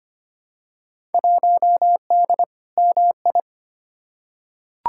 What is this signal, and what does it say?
Morse code, heard as a keyed tone, about 25 wpm, 700 Hz. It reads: 1D MI
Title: Field Day Exchanges - 25wpm